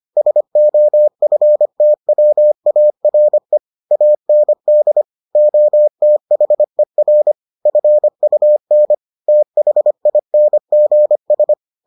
Morse code, heard as a keyed tone, about 25 wpm, 600 Hz. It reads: SOFTWARE AND OTHER FUN THINGS